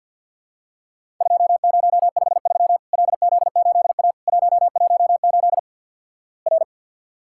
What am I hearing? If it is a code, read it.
Morse code, heard as a keyed tone, about 50 wpm, 700 Hz. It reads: 2053 LZ8A 118 R
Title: Worked All Europe DX Contest - QTC Exchange - 50wpm